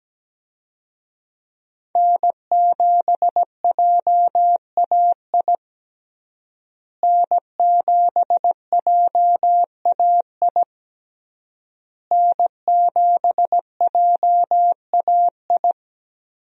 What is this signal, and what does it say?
Morse code, heard as a keyed tone, about 17 wpm, 700 Hz. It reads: N7JAI N7JAI N7JAI
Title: USA Call Signs - 5 Characters - Repeated 3x - 17wpm